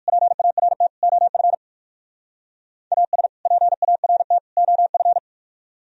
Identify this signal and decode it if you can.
Morse code, heard as a keyed tone, about 58 wpm, 700 Hz. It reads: PART OF AS PART OF